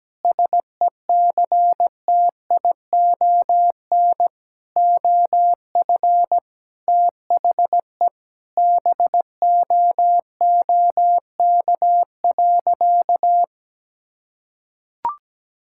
Morse code, heard as a keyed tone, about 17 wpm, 700 Hz. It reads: SECTION OF THE BOOK.
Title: Sentences from Top 1000 Words - Review Part 5 - 17wpm